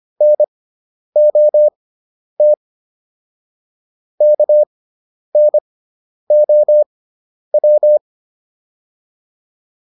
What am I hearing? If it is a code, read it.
Morse code, heard as a keyed tone, about 25 wpm, spaced as for 10 wpm, 600 Hz. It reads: NOT KNOW
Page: intermediate-practice-files